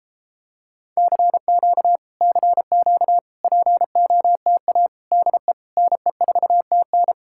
Morse code, heard as a keyed tone, about 29 wpm, 700 Hz. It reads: CQ CQ POTA DE NE4TN